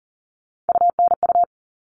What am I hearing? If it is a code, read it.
Morse code, heard as a keyed tone, about 40 wpm, 700 Hz. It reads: UNU
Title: All Cut Numbers - 3-Digit 40wpm